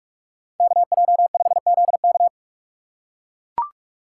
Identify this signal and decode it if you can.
Morse code, heard as a keyed tone, about 42 wpm, 700 Hz. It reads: KJ5ZK